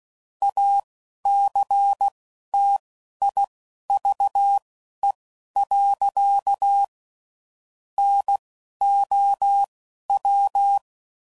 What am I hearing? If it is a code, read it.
Morse code, heard as a keyed tone, about 16 wpm, spaced as for 11 wpm, 800 Hz. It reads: ACTIVE. NOW
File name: CW Code Groups @10WPM / Technical Subjects / AND Gate